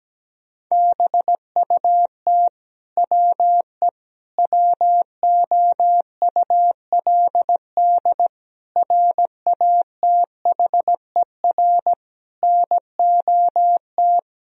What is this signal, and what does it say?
Morse code, heard as a keyed tone, about 17 wpm, 700 Hz. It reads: BUT WE WOULD RATHER NOT
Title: Sentences from Top 1000 Words - Review Part 4 - 17wpm